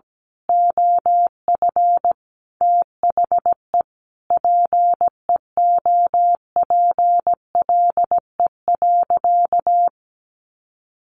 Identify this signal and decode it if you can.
Morse code, heard as a keyed tone, about 17 wpm, 700 Hz. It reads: OF THE PEOPLE.
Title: Sentences from Top 1500 Words 17wpm